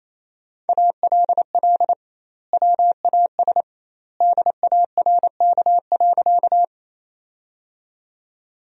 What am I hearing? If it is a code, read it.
Morse code, heard as a keyed tone, about 27 wpm, 700 Hz. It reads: ALL WAS DARK.
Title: 2x Word-Spacing - Sentences from Top 500 Words - 28wpm